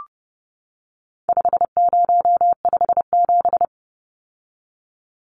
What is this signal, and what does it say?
Morse code, heard as a keyed tone, about 30 wpm, 700 Hz. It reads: 5057
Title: Numbers - 4-Digit 30wpm